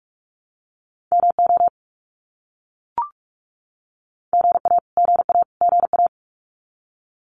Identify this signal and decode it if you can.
Morse code, heard as a keyed tone, about 44 wpm, 700 Hz. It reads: MO GA GA GA
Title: US States Repeated 3x 45wpm